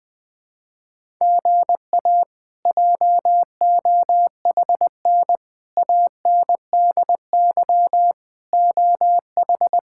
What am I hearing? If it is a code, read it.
Morse code, heard as a keyed tone, about 20 wpm, 700 Hz. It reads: GA JOHN ANDY OH
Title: K1USN SST QSO Practice - 20wpm